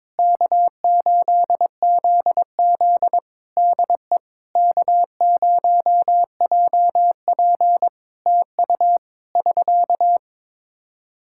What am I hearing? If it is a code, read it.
Morse code, heard as a keyed tone, about 22 wpm, 700 Hz. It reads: K8ZZ DE K0JP TU <SK>
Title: Practice QSO - Exchange 4 - Ending - 22wpm